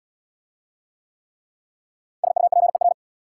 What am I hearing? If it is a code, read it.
Morse code, heard as a keyed tone, about 69 wpm, 700 Hz. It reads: SUPER